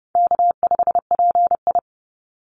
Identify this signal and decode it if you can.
Morse code, heard as a keyed tone, about 30 wpm, 700 Hz. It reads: K5PI
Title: Easy Call Signs Repeated 3x - 30wpm